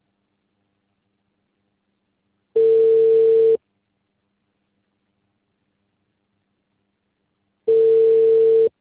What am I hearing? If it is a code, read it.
Ringback tone (China, 450 Hz).